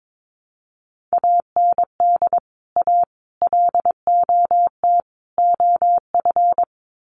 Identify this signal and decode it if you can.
Morse code, heard as a keyed tone, about 22 wpm, 700 Hz. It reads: AND A LOT OF
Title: Sets of 4 Words 22wpm